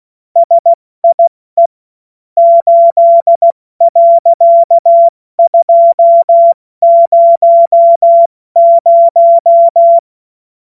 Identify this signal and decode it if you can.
Morse code, heard as a keyed tone, about 16 wpm, 680 Hz. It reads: SIE 8.200
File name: Morsekurs / Witze / Witz 28. Dezember 2021, 70 F80